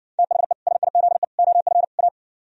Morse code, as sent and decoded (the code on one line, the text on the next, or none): - .... / ... .. --.. / --- ..-. / .-
THE SIZE OF A